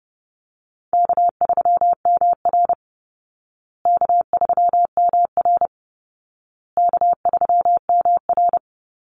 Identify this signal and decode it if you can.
Morse code, heard as a keyed tone, about 30 wpm, 700 Hz. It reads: K3MR K3MR K3MR
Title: Easy Call Signs Repeated 3x - 30wpm